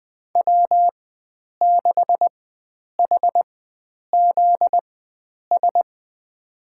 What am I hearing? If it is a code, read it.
Morse code, heard as a keyed tone, about 20 wpm, spaced as for 9 wpm, 700 Hz. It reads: W6HZS
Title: USA Call Signs - 5 Characters - 4x Character Spacing - 20wpm